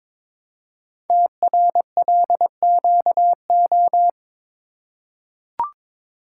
Morse code, heard as a keyed tone, about 22 wpm, 700 Hz. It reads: TRLQO
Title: Code Groups - 5 Letter:Numbers - 22wpm